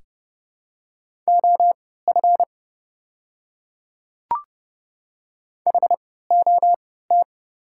Morse code, heard as a keyed tone, about 31 wpm, spaced as for 17 wpm, 700 Hz. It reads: OF HOT
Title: 3x Character Spacing - Top 200 Words - 30wpm